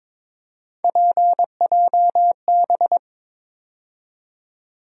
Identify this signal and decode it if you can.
Morse code, heard as a keyed tone, about 22 wpm, 700 Hz. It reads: PJB
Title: USA Call Sign Postfixes - 3 Letters 22wpm